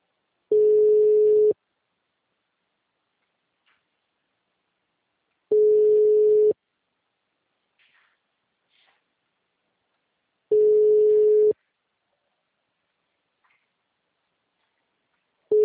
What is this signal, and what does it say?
Ringback tone (Europe, 425 Hz).
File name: out- -2008- - - .96661